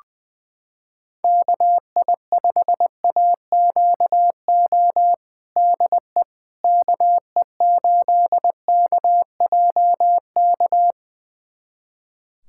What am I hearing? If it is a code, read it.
Morse code, heard as a keyed tone, about 21 wpm, 700 Hz. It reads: KI5AQO DE KE8KJK
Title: USA Callsign DE Callsign 20wpm